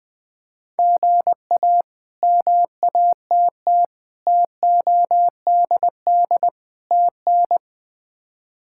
Morse code, heard as a keyed tone, about 20 wpm, 700 Hz. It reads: GA MATT TODD TN
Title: K1USN SST QSO Practice - 20wpm